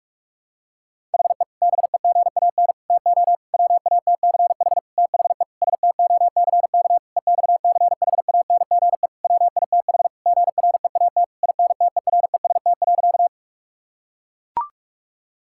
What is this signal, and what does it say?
Morse code, heard as a keyed tone, about 41 wpm, 700 Hz. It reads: HE BEGAN TO WATCH THE STOCK EXCHANGE WITH GREAT INTEREST.
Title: Sentences from Top 1000 Words - Review Part 6 - 45wpm